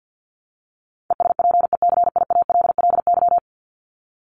Morse code, heard as a keyed tone, about 50 wpm, 700 Hz. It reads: ESPECIALLY
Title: Top 501-750 Words 50wpm